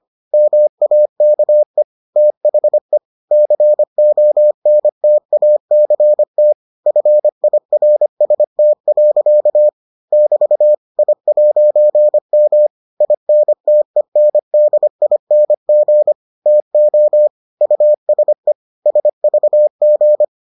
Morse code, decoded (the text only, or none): MAKE THE CONTACT FIRST. <BT> I'M INTENDING TO USE SVG